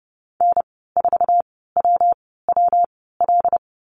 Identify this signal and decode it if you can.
Morse code, heard as a keyed tone, about 30 wpm, spaced as for 17 wpm, 700 Hz. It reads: N4WWL
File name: Lesson 19g - Letter w - Call Signs 30wpm